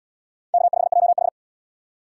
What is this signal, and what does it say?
Morse code, heard as a keyed tone, about 75 wpm, 700 Hz. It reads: K5JH